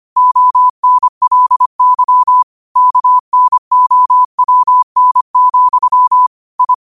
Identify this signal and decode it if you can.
Morse code, heard as a keyed tone, about 25 wpm, 1000 Hz. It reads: ONLY KNOWN, I
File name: qotdfiles / 25wpm